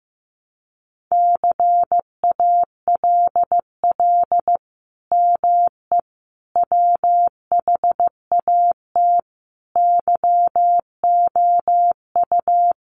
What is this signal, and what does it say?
Morse code, heard as a keyed tone, about 15 wpm, 700 Hz. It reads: CALL ME WHAT YOU
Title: Sentences from Top 200 Words 15wpm